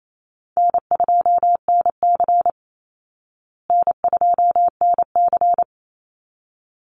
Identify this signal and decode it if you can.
Morse code, heard as a keyed tone, about 28 wpm, 700 Hz. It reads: N2NC N2NC
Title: Easy Call Signs Repeated 2x - 28wpm